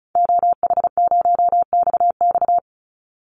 Morse code, heard as a keyed tone, about 34 wpm, 700 Hz. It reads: OH0XX
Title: Call Signs Repeated 3x - 35wpm